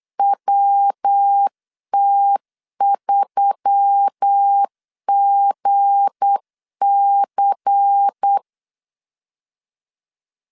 Morse code, heard as a keyed tone, about 8 wpm, 800 Hz. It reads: WT3GC
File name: LEZIONI / 7 GF9Z / 7 G 8wpm 3118